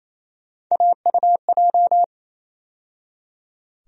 Morse code, heard as a keyed tone, about 28 wpm, 700 Hz. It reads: AUJ
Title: USA Call Sign Postfixes - 3 Letters 28wpm